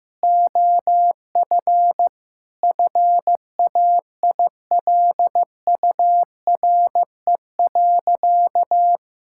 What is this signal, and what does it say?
Morse code, heard as a keyed tone, about 15 wpm, 700 Hz. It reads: OF FAILURE.
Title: Sentences from Top 1000 Words - Review Part 6 - 15wpm